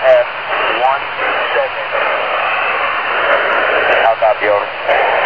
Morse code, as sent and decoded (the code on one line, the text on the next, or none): none